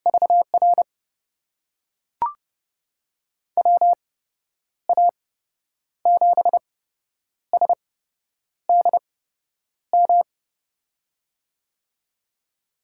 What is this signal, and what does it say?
Morse code, heard as a keyed tone, about 32 wpm, spaced as for 9 wpm, 700 Hz. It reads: VR WA7SDM
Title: USA Call Signs - 6 Characters - 8x Character Spacing - 30wpm